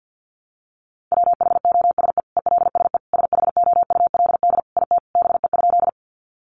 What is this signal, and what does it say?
Morse code, heard as a keyed tone, about 50 wpm, 700 Hz. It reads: WHOSE ELSE SHOULD IT BE?
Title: Sentences from Top 1200 Words 50wpm